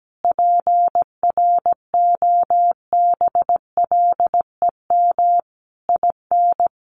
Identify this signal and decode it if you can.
Morse code, heard as a keyed tone, about 17 wpm, 700 Hz. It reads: PROBLEM IN
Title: Sentences from Top 1400 Words 17wpm